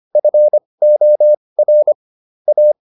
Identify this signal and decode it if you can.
Morse code, heard as a keyed tone, about 25 wpm, spaced as for 20 wpm, 600 Hz. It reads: FOR A